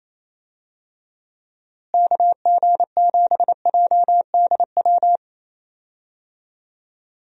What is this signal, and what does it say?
Morse code, heard as a keyed tone, about 28 wpm, 700 Hz. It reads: KG7JDW
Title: USA Call Signs - 6 Characters - Repeated 2x - 28wpm